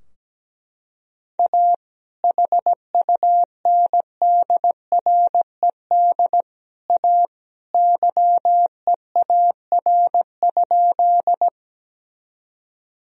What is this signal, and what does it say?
Morse code, heard as a keyed tone, about 17 wpm, 700 Hz. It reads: A HUNDRED A YEAR?
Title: Sentences from Top 1000 Words - Review Part 2 - 17wpm